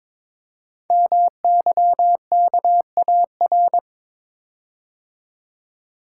Morse code, heard as a keyed tone, about 23 wpm, 700 Hz. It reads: MYKAR
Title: Code Groups - 5 Letters - 22wpm